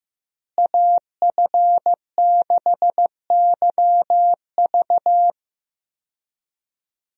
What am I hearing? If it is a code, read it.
Morse code, heard as a keyed tone, about 15 wpm, 700 Hz. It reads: AF6YV
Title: USA Call Signs - 5 Characters - Repeated 2x - 15wpm